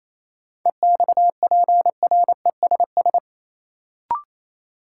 Morse code, heard as a keyed tone, about 28 wpm, 700 Hz. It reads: EXPRESS
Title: Sentences from Top 1100 Words - New Word Drill Alt - 28wpm